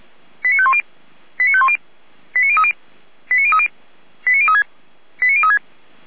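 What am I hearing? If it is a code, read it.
Five-tone selective call (ZVEI), heard as quick runs of five tones, tones of about 70 ms. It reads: 89510 89510 89030 89030 89036 89036
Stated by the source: ZVEI five-tone call 89140 received via SDR, fire brigade Feuerwehr Orlamünde.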